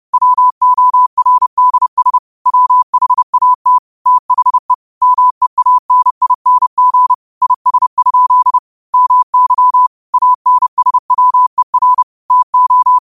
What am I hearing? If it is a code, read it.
Morse code, heard as a keyed tone, about 30 wpm, 1000 Hz. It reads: WORDS WHAT THE MEANING IS? MY ANSWER TO
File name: qotdfiles / 30wpm